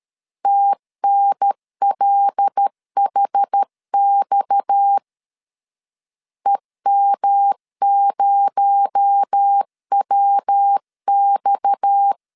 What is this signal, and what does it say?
Morse code, heard as a keyed tone, about 13 wpm, 800 Hz. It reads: TNLHX EM0WX
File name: LEZIONI / 7 GF9Z / 7 GF 12wpm 2122